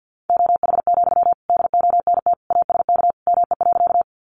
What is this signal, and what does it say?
Morse code, heard as a keyed tone, about 50 wpm, 700 Hz. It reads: OH, DONT ASK ME.
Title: Sentences from Top 500 Words 50wpm